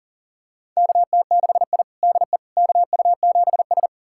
Morse code, heard as a keyed tone, about 40 wpm, 700 Hz. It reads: KT6I DE KU7S
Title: USA Callsign DE Callsign - 4 Characters - 40wpm